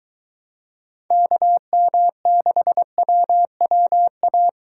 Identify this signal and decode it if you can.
Morse code, heard as a keyed tone, about 23 wpm, 700 Hz. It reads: KM6WWA